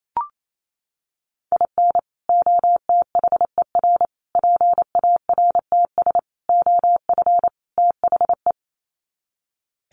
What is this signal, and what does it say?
Morse code, heard as a keyed tone, about 28 wpm, 700 Hz. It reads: IN OTHER PARTS OF THE
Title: Sets of 5 Words 28wpm